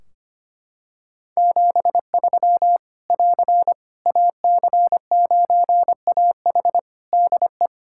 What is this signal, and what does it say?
Morse code, heard as a keyed tone, about 25 wpm, 700 Hz. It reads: 73 <AR> AC9AH DE
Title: K1USN SST QSO Practice 25wpm